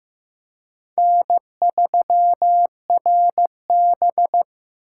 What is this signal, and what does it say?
Morse code, heard as a keyed tone, about 15 wpm, 700 Hz. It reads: N3RB
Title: USA Call Signs - 4 Characters - 2x Character Spacing - 15wpm